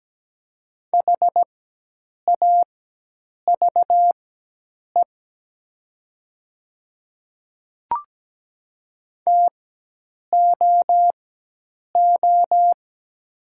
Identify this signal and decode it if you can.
Morse code, heard as a keyed tone, about 17 wpm, spaced as for 8 wpm, 700 Hz. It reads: HAVE TOO
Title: 4x Character Spacing - Top 200 Words - 17wpm